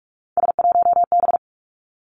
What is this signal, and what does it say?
Morse code, heard as a keyed tone, about 46 wpm, 700 Hz. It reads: S1B
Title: Three Challenging Letter-Number Combinations 45wpm